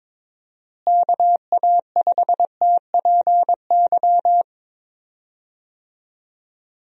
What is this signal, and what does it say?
Morse code, heard as a keyed tone, about 22 wpm, 700 Hz. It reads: KA5TPY
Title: USA Call Signs - 6 Characters - Repeated 3x - 22wpm